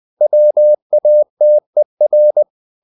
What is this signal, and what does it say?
Morse code, heard as a keyed tone, about 20 wpm, 600 Hz. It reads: WATER
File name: the old man and the sea part 2 20wpm